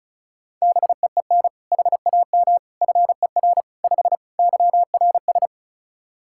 Morse code, heard as a keyed tone, about 34 wpm, 700 Hz. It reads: BEEN HAM FER 5 YRS
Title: K1USN SST QSO Practice 35wpm